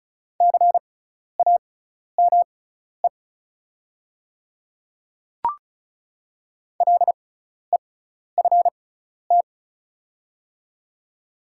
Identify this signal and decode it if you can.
Morse code, heard as a keyed tone, about 36 wpm, spaced as for 12 wpm, 700 Hz. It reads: CAME LEFT
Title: 6x Character Spacing - Top 500 Words - 35wpm